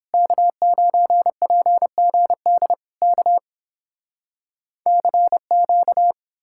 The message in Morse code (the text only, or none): K9PGD K CQ